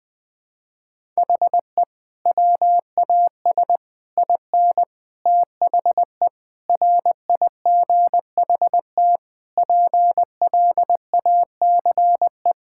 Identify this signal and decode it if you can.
Morse code, heard as a keyed tone, about 21 wpm, 700 Hz. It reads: HE WAS IN THE RIGHT PLACE
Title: Sentences with 6-Word Phrases - 20wpm